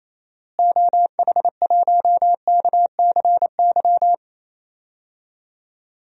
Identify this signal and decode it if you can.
Morse code, heard as a keyed tone, about 28 wpm, 700 Hz. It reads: OH1KCY
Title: DXCC-100 Call Signs - 6 Characters - Repeated 3x - 28wpm